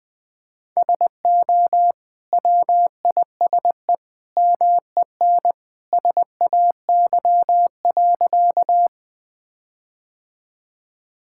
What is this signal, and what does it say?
Morse code, heard as a keyed tone, about 20 wpm, 700 Hz. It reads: SO WISE MEN SAY.
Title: Sentences from Top 1600 Words 20wpm